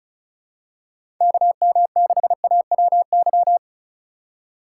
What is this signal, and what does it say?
Morse code, heard as a keyed tone, about 34 wpm, 700 Hz. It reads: KM6AWY
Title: USA Call Signs - 6 Characters - 4x Character Spacing - 35wpm